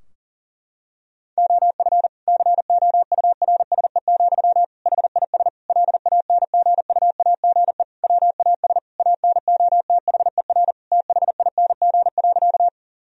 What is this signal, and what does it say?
Morse code, heard as a keyed tone, about 40 wpm, 700 Hz. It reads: OF COURSE, HIS LANGUAGE WAS ANOTHER THING.
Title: Sentences from Top 1000 Words - Review Part 3 - 40wpm